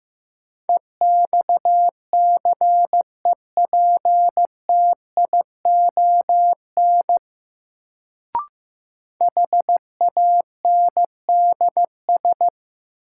Morse code, heard as a keyed tone, about 15 wpm, 700 Hz. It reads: EXCEPTION HANDS
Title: Sets of 4 Words - New Word Drill - 15wpm